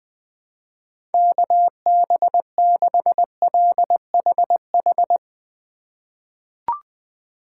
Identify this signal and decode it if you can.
Morse code, heard as a keyed tone, about 20 wpm, 700 Hz. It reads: KB6LHH
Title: USA Call Signs - 6 Characters - 4x Character Spacing - 20wpm